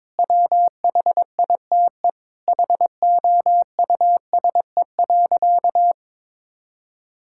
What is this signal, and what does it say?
Morse code, heard as a keyed tone, about 22 wpm, 700 Hz. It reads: WHITE HOUSE.
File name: practice / 22wpm / Sentences with 8-Word Phrases - 22wpm